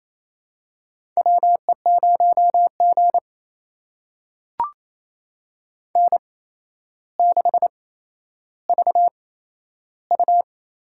Morse code, heard as a keyed tone, about 28 wpm, spaced as for 12 wpm, 700 Hz. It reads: WE0G N6VU
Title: USA Call Signs - 8x Character Spacing - 28wpm